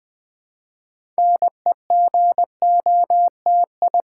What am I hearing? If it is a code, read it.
Morse code, heard as a keyed tone, about 20 wpm, 700 Hz. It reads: NEGOTI
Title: Build A Word - 3 Letters At A Time - 9-Letter Words - 20wpm